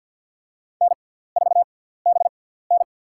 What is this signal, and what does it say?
Morse code, heard as a keyed tone, about 53 wpm, spaced as for 17 wpm, 700 Hz. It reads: N4BN
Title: USA Call Signs - 4 Characters - 6x Character Spacing - 50wpm